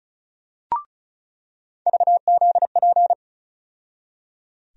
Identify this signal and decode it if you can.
Morse code, heard as a keyed tone, about 36 wpm, 700 Hz. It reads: VZP
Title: USA Call Sign Postfixes - 3 Letters 35wpm